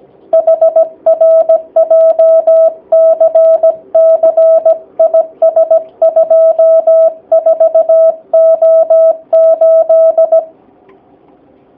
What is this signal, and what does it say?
Morse code, heard as a keyed tone, about 17 wpm, 650 Hz. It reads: HR JCC IS 24O8
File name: jh8cbh / 2019cw / sp16- w